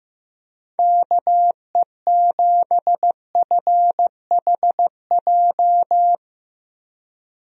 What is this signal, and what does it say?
Morse code, heard as a keyed tone, about 15 wpm, 700 Hz. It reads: KE7FHJ
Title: USA Call Signs - 6 Characters - Repeated 2x - 15wpm